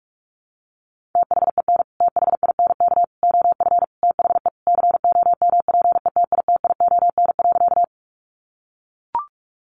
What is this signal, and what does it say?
Morse code, heard as a keyed tone, about 44 wpm, 700 Hz. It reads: THEN THINK OF THE COMPETITION.
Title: Sentences from Top 1000 Words 45wpm